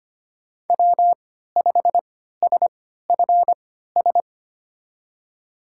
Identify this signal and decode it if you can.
Morse code, heard as a keyed tone, about 25 wpm, spaced as for 14 wpm, 700 Hz. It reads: W5SFS
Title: USA Call Signs - 5 Characters - 3x Character Spacing - 25wpm